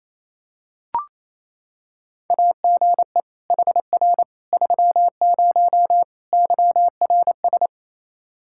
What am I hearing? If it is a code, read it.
Morse code, heard as a keyed tone, about 28 wpm, 700 Hz. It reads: AGE HR 30 YRS
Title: Practice QSO - Exchange 3 - More Info - 28wpm